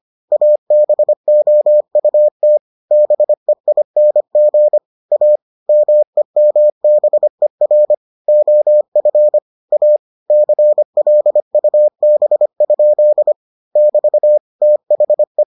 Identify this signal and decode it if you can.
Morse code, heard as a keyed tone, about 25 wpm, 600 Hz. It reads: ABOUT BEING A MEMBER OF A CLUB? <BT> THE